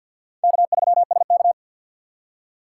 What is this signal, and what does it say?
Morse code, heard as a keyed tone, about 52 wpm, 700 Hz. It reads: K3SK
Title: USA Call Signs - 2x Character Spacing - 50wpm